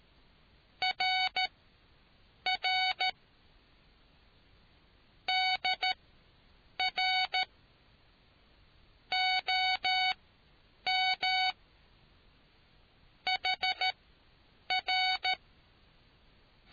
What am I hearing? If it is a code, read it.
Morse code, heard as a keyed tone, about 13 wpm, spaced as for 7 wpm, 750 Hz. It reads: RR DR OM HR